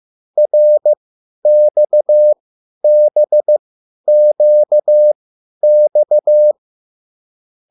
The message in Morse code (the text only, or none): RXBQX